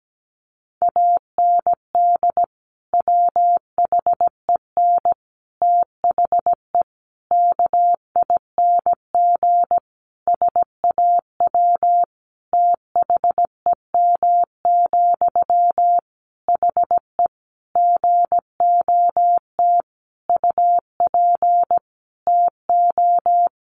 Morse code, heard as a keyed tone, about 17 wpm, 700 Hz. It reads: AND WHEN THE KING SAW THEM, HE GOT UP TO
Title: Sentences from Top 1100 Words 17wpm